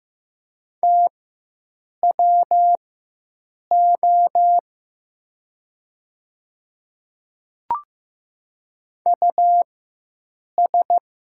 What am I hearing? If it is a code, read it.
Morse code, heard as a keyed tone, about 15 wpm, spaced as for 7 wpm, 700 Hz. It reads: TWO US
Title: 4x Character Spacing - Top 200 Words - 15wpm